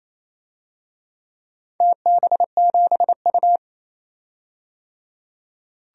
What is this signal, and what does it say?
Morse code, heard as a keyed tone, about 28 wpm, 700 Hz. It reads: TB7U
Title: DXCC-100 Call Signs - 4 Characters - 28wpm